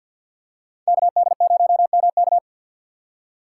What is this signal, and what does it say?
Morse code, heard as a keyed tone, about 53 wpm, 700 Hz. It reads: KD0MK
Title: USA Call Signs - 5 Characters - 2x Character Spacing - 50wpm